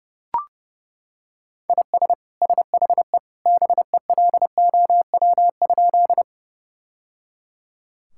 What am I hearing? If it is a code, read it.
Morse code, heard as a keyed tone, about 30 wpm, 700 Hz. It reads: IS SHE BELOW?
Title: Sentences from Top 1000 Words - Review Part 5 - 30wpm